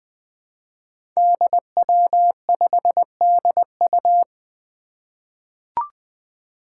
Morse code, heard as a keyed tone, about 20 wpm, 700 Hz. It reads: DW5DU
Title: Five Challenging Letter-Number Combinations 20wpm